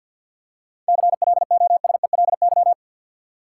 Morse code, heard as a keyed tone, about 52 wpm, 700 Hz. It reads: CLOSELY
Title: Seven Letter Words - 50wpm